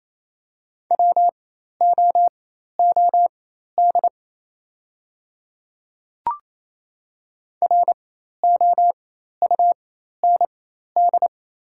Morse code, heard as a keyed tone, about 28 wpm, spaced as for 13 wpm, 700 Hz. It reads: WOOD ROUND
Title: 4x Character Spacing - Top 300 Words - 28wpm